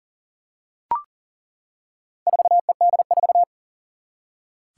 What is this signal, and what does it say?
Morse code, heard as a keyed tone, about 40 wpm, 700 Hz. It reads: 4ED4
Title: Four Challenging Letter-Number Combinations 40wpm